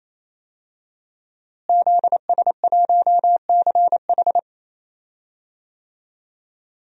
Morse code, heard as a keyed tone, about 27 wpm, 700 Hz. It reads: ZS1CH
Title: DXCC-100 Call Signs - 5 Characters - Repeated 3x - 28wpm